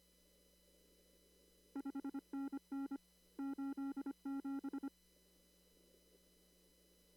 Morse code, heard as a keyed tone, about 25 wpm, 280 Hz. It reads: 5NN 87
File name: CW / kt5j wpx cw-op-k5pi